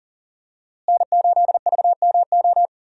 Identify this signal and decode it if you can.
Morse code, heard as a keyed tone, about 40 wpm, 700 Hz. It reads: N8VMO
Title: USA Call Signs - 5 Characters - Repeated 3x - 40wpm